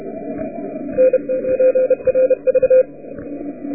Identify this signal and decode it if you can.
Morse code, heard as a keyed tone, about 29 wpm, 500 Hz. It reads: N PRV